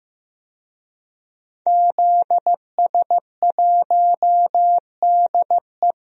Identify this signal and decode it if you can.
Morse code, heard as a keyed tone, about 15 wpm, 700 Hz. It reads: ZS1DE
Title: DXCC-100 Call Signs - 5 Characters - 15wpm